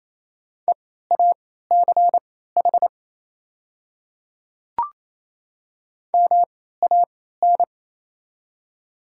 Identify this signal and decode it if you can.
Morse code, heard as a keyed tone, about 28 wpm, spaced as for 16 wpm, 700 Hz. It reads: EACH MAN
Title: 3x Character Spacing - Top 200 Words - 28wpm